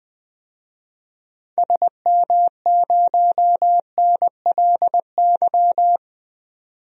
Morse code, heard as a keyed tone, about 20 wpm, 700 Hz. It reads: SM0NLY
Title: DXCC-100 Call Signs - 6 Characters - 20wpm